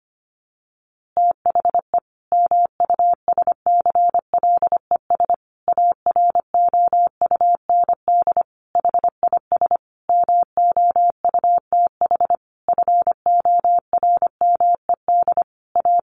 Morse code, heard as a keyed tone, about 25 wpm, 700 Hz. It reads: THE MUSCLES AROUND HIS MOUTH FORMED A
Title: A Sound Of Thunder 25wpm